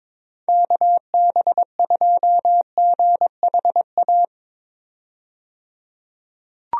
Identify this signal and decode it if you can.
Morse code, heard as a keyed tone, about 21 wpm, 700 Hz. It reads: KB2GHA
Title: USA Call Signs - 6 Characters - 22wpm